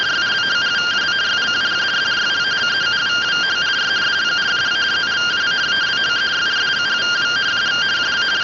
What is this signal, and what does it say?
Two-tone teleprinter signal, about 170 Hz shift.
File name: findings / CT 205 ax mips, iDose 3 0006.gen-07 rtty rx